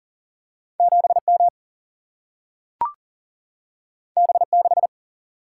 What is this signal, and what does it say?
Morse code, heard as a keyed tone, about 40 wpm, 700 Hz. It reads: 7M B6
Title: Single-Letter Single-Number 40wpm